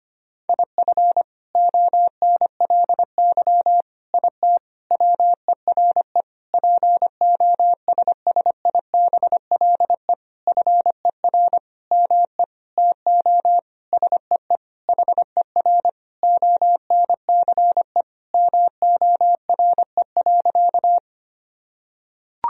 Morse code, heard as a keyed tone, about 25 wpm, 700 Hz. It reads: IF ONLY IT WERE POSSIBLE FER ME TO SEE HER ONCE MORE.
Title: Sentences from Top 500 Words and Abbreviations 25wpm